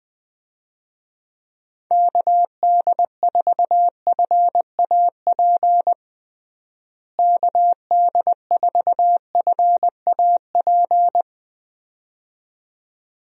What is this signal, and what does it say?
Morse code, heard as a keyed tone, about 21 wpm, 700 Hz. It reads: KD4FAP KD4FAP
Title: USA Call Signs - 6 Characters - Repeated 2x - 20wpm